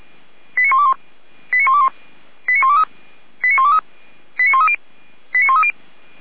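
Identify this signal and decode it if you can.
Five-tone selective call (ZVEI), heard as quick runs of five tones, tones of about 70 ms. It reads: 89121 89121 89123 89123 89130 89130
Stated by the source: ZVEI five-tone call received on SDR, code 89022, archived by fire brigade Feuerwehr Orlamünde.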